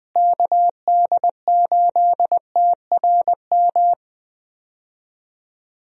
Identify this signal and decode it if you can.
Morse code, heard as a keyed tone, about 20 wpm, 700 Hz. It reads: KD8TRM
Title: USA Call Signs - 6 Characters - Repeated 2x - 20wpm